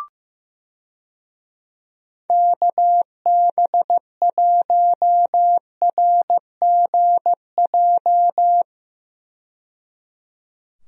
Morse code, heard as a keyed tone, about 15 wpm, 700 Hz. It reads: KB1RGJ
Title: USA Call Signs - 6 Characters - 15wpm